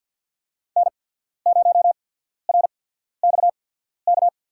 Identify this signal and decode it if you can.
Morse code, heard as a keyed tone, about 53 wpm, spaced as for 14 wpm, 700 Hz. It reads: N0RXK
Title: USA Call Signs - 5 Characters - 8x Character Spacing - 50wpm